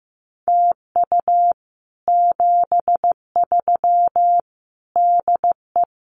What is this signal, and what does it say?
Morse code, heard as a keyed tone, about 14 wpm, 700 Hz. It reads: TU 73 DE
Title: Common QSO Phrases Encore 15wpm